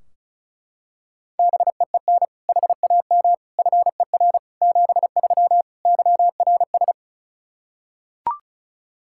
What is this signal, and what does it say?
Morse code, heard as a keyed tone, about 34 wpm, 700 Hz. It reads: BEEN HAM FER 73 YRS
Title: K1USN SST QSO Practice 35wpm